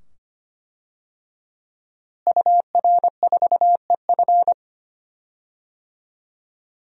Morse code, heard as a keyed tone, about 25 wpm, 700 Hz. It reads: UR4EF
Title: DXCC-100 Call Signs - 5 Characters - Repeated 2x - 25wpm